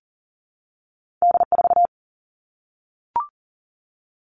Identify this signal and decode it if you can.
Morse code, heard as a keyed tone, about 40 wpm, 700 Hz. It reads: D4
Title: All Cut Numbers - 2-Digit 40wpm